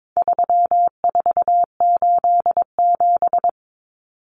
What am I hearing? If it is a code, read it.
Morse code, heard as a keyed tone, about 22 wpm, 700 Hz. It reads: 3487
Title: Numbers - 4-Digit 22wpm